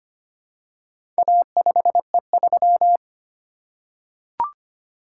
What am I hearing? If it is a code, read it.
Morse code, heard as a keyed tone, about 25 wpm, 700 Hz. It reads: A5E3
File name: practice / 25wpm / Four Letter-Number Combinations - 25wpm